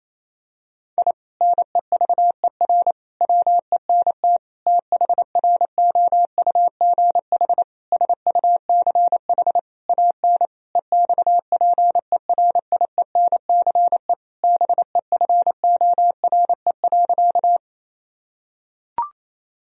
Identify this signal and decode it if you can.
Morse code, heard as a keyed tone, about 28 wpm, 700 Hz. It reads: I NEVER WENT THROUGH SUCH AN EXPERIENCE BEFORE.
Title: Sentences from Top 1000 Words - Review Part 3 - 28wpm